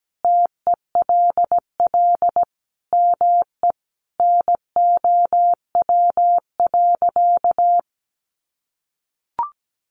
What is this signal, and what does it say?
Morse code, heard as a keyed tone, about 17 wpm, 700 Hz. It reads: TELL ME NOW.
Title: Sentences from Top 1100 Words 17wpm